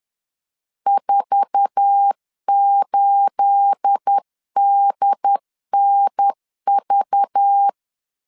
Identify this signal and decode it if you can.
Morse code, heard as a keyed tone, about 10 wpm, 800 Hz. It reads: 48DNV
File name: LEZIONI / 9 28 10wpm 2842